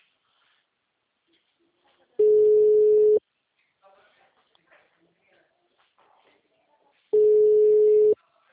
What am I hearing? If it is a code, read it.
Ringback tone (Europe, 425 Hz), with speech.